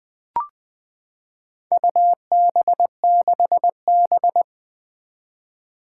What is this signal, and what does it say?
Morse code, heard as a keyed tone, about 20 wpm, 700 Hz. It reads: UB6B
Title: Four Challenging Letter-Number Combinations 20wpm